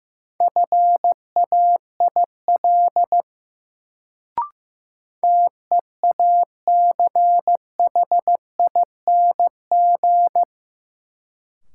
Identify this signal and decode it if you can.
Morse code, heard as a keyed tone, about 15 wpm, 700 Hz. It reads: FAIL TEACHING
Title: Sentences from Top 1100 Words - New Word Drill Alt - 15wpm